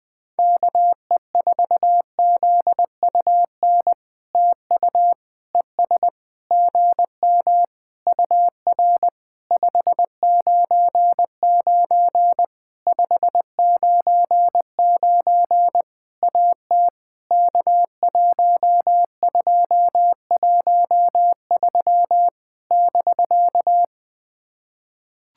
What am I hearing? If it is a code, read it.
Morse code, heard as a keyed tone, about 20 wpm, 700 Hz. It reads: KE4ZUN TU ES GM UR 599 599 AT K1213 <BK>
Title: POTA QSO Practice - 20wpm